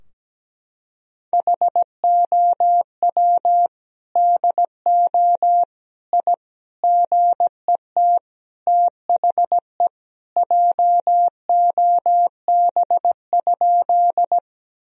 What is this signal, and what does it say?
Morse code, heard as a keyed tone, about 17 wpm, 700 Hz. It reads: HOW DO I GET THE JOB?
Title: Sentences from Top 1000 Words - Review Part 3 - 17wpm